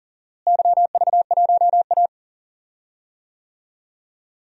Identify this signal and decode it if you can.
Morse code, heard as a keyed tone, about 40 wpm, 700 Hz. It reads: YV1A
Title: DXCC-100 Call Signs - 4 Characters - Repeated 2x - 40wpm